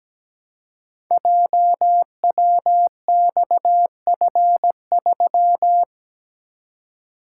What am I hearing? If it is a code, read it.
Morse code, heard as a keyed tone, about 17 wpm, 700 Hz. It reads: JWXF3
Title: Code Groups - 5 Letter:Numbers - 17wpm